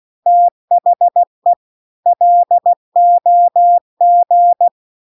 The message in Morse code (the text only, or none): THE LOG